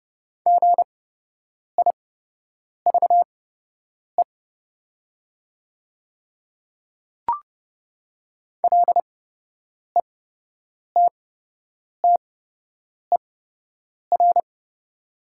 Morse code, heard as a keyed tone, about 30 wpm, spaced as for 8 wpm, 700 Hz. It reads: GIVE LETTER
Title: 8x Character Spacing - Top 300 Words - 30wpm